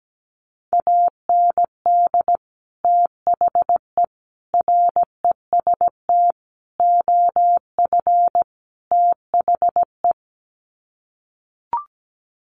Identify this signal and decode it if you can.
Morse code, heard as a keyed tone, about 17 wpm, 700 Hz. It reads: AND THE REST OF THE
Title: Sets of 5 Words 17wpm